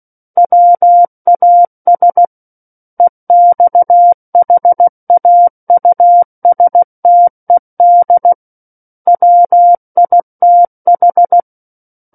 Morse code, heard as a keyed tone, about 16 wpm, 700 Hz. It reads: WAS EXHAUSTED WITH
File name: Book One / 07-Chapter 7 - 16 WPM